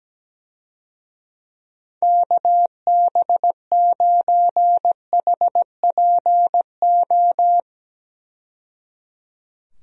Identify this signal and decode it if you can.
Morse code, heard as a keyed tone, about 17 wpm, 700 Hz. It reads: KB9HPO
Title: USA Call Signs - 6 Characters - 17wpm